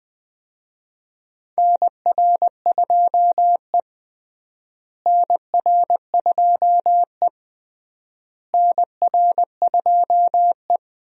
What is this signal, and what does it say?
Morse code, heard as a keyed tone, about 20 wpm, 700 Hz. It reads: NR2E NR2E NR2E
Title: USA Call Signs - 4 Characters - Repeated 3x - 20wpm